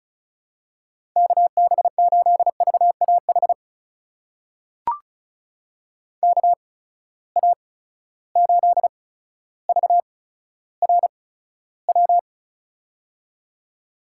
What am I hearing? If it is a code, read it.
Morse code, heard as a keyed tone, about 36 wpm, spaced as for 15 wpm, 700 Hz. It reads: KB8VAH KA8VRW
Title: USA Call Signs - 6 Characters - 8x Character Spacing - 35wpm